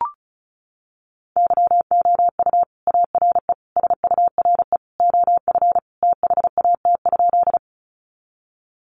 Morse code, heard as a keyed tone, about 34 wpm, 700 Hz. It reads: YOU ARE SURE OF THAT?
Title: Sentences from Top 300 Words 35wpm